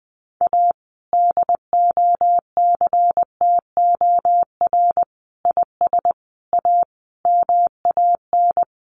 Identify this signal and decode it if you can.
Morse code, heard as a keyed tone, about 20 wpm, 700 Hz. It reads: A DOCTOR IS A MAN
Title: Sentences from Top 1500 Words 20wpm